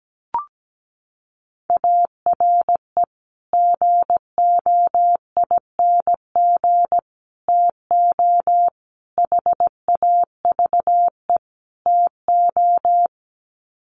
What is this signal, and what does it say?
Morse code, heard as a keyed tone, about 17 wpm, 700 Hz. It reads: ARE GOING TO HAVE TO
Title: Sets of 5 Words 17wpm